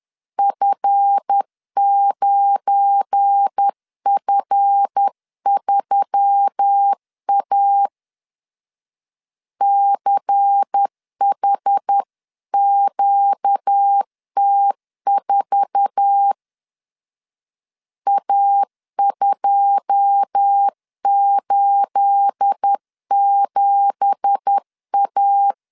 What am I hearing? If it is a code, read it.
Morse code, heard as a keyed tone, about 10 wpm, 800 Hz. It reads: F9F3A CHQT4 A287A
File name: CW / DET / 1 9 1500C S A 10wpm 4511